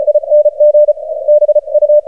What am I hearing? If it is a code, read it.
Morse code, heard as a keyed tone, about 38 wpm, spaced as for 33 wpm, 600 Hz. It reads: ING BU
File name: archived / CRSnet